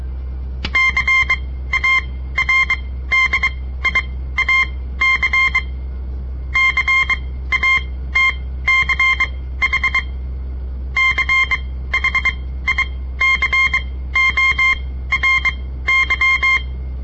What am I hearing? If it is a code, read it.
Morse code, heard as a keyed tone, about 23 wpm, spaced as for 15 wpm, 2100 Hz. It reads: CARDIAC CATCH CHICORY
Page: On the air Morse code practice from the Southern Montana Amatuer Radio Association; all code, no read back